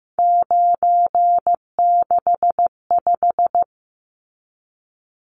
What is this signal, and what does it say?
Morse code, heard as a keyed tone, about 15 wpm, 700 Hz. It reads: 965
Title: Numbers - 3-Digit 15wpm